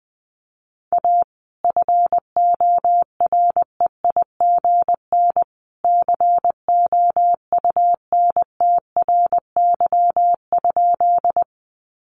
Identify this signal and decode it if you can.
Morse code, heard as a keyed tone, about 19 wpm, 700 Hz. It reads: A FOREIGN COUNTRY?
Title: Sentences from Top 1400 Words 20wpm